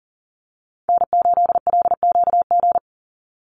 Morse code, heard as a keyed tone, about 40 wpm, 700 Hz. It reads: N8LQG
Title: USA Call Signs 40wpm